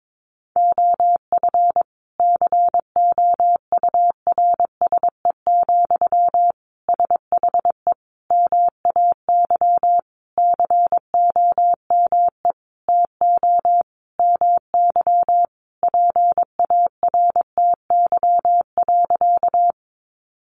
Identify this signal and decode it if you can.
Morse code, heard as a keyed tone, about 22 wpm, 700 Hz. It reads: OF COURSE, SHE MAY COME TO MY PARTY.
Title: Sentences from Top 1400 Words 22wpm